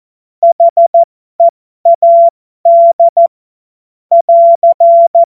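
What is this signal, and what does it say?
Morse code, heard as a keyed tone, about 14 wpm, spaced as for 12 wpm, 680 Hz. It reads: HEAD <AR>
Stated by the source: Morse code joke.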